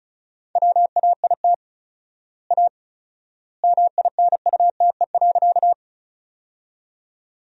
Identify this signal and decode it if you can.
Morse code, heard as a keyed tone, about 36 wpm, 700 Hz. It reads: WAIT A MINUTE.
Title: 4x Word-Spacing - Sentences from Top 500 Words - 35wpm